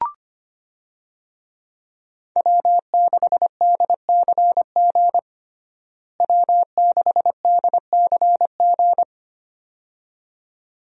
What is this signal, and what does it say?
Morse code, heard as a keyed tone, about 25 wpm, 700 Hz. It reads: W6DCG W6DCG
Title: USA Call Signs - 5 Characters - Repeated 2x - 25wpm